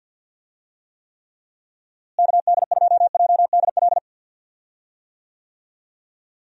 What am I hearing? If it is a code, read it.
Morse code, heard as a keyed tone, about 46 wpm, 700 Hz. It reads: KD2JDL
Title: USA Call Signs - 6 Characters - Repeated 3x - 50wpm